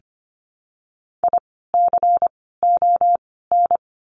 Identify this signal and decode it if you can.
Morse code, heard as a keyed tone, about 25 wpm, 700 Hz. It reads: I C O N
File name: course / 25:10wpm / 094 Lesson 16f - Letter c - Words 25wpm